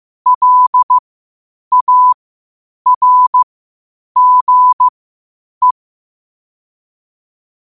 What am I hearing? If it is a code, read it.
Morse code, heard as a keyed tone, about 15 wpm, spaced as for 8 wpm, 1000 Hz. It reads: LARGE
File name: qotdfiles / 8wpm